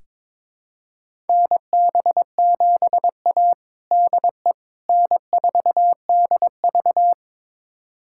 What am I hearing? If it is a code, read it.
Morse code, heard as a keyed tone, about 22 wpm, 700 Hz. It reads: NB7A DE N4DV
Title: USA Callsign DE Callsign - 4 Characters - 22wpm